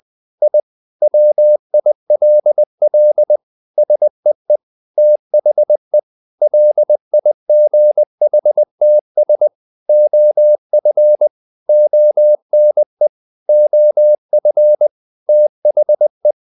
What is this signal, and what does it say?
Morse code, heard as a keyed tone, about 20 wpm, 600 Hz. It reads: I WILL SEE THE LIGHTS OF ONE OF THE
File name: the old man and the sea part 2 20wpm